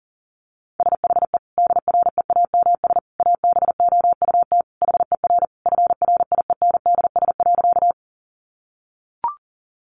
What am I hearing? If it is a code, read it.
Morse code, heard as a keyed tone, about 40 wpm, 700 Hz. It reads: SHE DREAMS ABOUT HER FRIENDS.
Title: Easy Phrases 40wpm